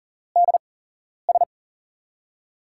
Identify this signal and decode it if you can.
Morse code, heard as a keyed tone, about 40 wpm, spaced as for 11 wpm, 700 Hz. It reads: DS
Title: USA Call Signs - 4 Characters - 8x Character Spacing - 40wpm